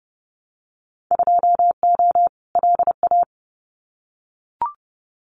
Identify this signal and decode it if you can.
Morse code, heard as a keyed tone, about 30 wpm, 700 Hz. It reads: 2O LA
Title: Winter Field Day Exchanges 30wpm